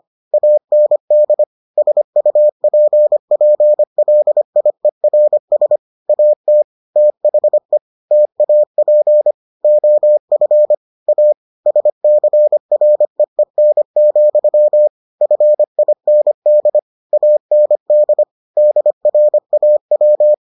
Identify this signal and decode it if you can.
Morse code, heard as a keyed tone, about 25 wpm, 600 Hz. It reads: AND SUPPLIERS AT THE TAP OF A SCREEN, FIND AND DRAW